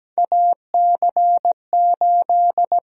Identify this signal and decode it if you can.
Morse code, heard as a keyed tone, about 17 wpm, 700 Hz. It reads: AC8
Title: USA Call Sign Prefixes - 17wpm